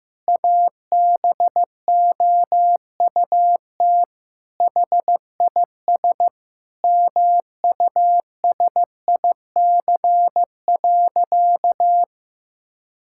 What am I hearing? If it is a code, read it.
Morse code, heard as a keyed tone, about 15 wpm, 700 Hz. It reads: ABOUT HIS MUSIC.
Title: Sentences from Top 1000 Words - Review Part 6 - 15wpm